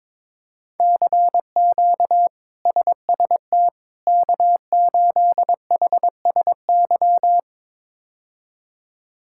Morse code, heard as a keyed tone, about 22 wpm, 700 Hz. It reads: CQ SST K8HSY
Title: K1USN SST QSO Practice - 22wpm